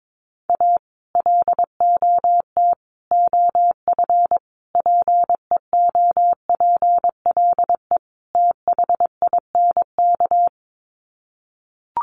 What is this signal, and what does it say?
Morse code, heard as a keyed tone, about 22 wpm, 700 Hz. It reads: A LOT OF PEOPLE THINK
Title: Sets of 5 Words Encore 22wpm